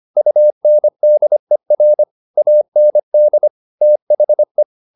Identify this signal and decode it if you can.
Morse code, heard as a keyed tone, about 25 wpm, 600 Hz. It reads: UNDER AND THE
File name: the old man and the sea part 1 25wpm